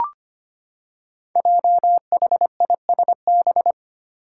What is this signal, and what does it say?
Morse code, heard as a keyed tone, about 25 wpm, 700 Hz. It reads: JHISB
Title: Five Challenging Letter-Number Combinations 25wpm